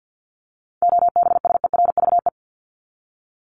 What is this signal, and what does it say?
Morse code, heard as a keyed tone, about 50 wpm, 700 Hz. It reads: OBSERVE
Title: ED Suffix - Top 100 Words with Base Words 50wpm